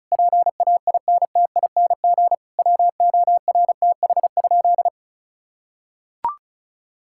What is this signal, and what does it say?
Morse code, heard as a keyed tone, about 36 wpm, 690 Hz. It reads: PAINTING WORTH?
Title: Sentences from Top 1000 Words - Review Part 6 - 35wpm